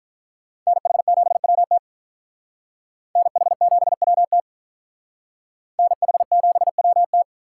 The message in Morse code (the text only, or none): NH7WT NH7WT NH7WT